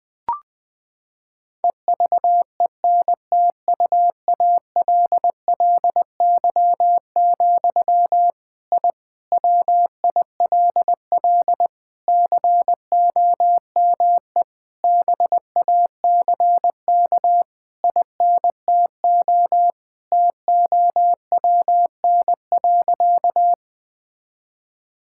Morse code, heard as a keyed tone, about 20 wpm, 700 Hz. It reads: EVENTUALLY, I WILL COME BACK INTO TOWN.
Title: Sentences from Top 1000 Words - Review Part 6 - 20wpm